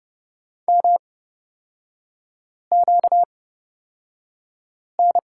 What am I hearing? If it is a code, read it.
Morse code, heard as a keyed tone, about 30 wpm, spaced as for 5 wpm, 700 Hz. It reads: MQN
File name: Lesson 38e - Letter q - Three Characters 30wpm